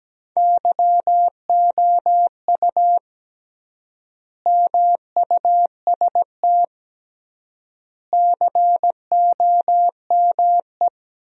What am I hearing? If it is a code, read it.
Morse code, heard as a keyed tone, about 17 wpm, 700 Hz. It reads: YOU MUST COME
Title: 3x Word-Spacing - Sentences from Top 200 Words - 17wpm